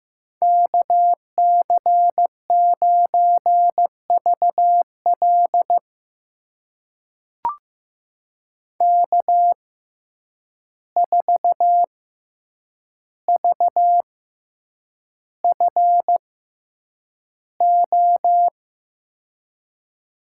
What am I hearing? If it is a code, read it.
Morse code, heard as a keyed tone, about 15 wpm, spaced as for 8 wpm, 700 Hz. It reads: KC9VL K4VFO
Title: USA Call Signs - 5 Characters - 6x Character Spacing - 15wpm